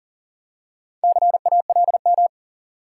Morse code, heard as a keyed tone, about 39 wpm, 700 Hz. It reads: CALM
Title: Four Letter Words - 40wpm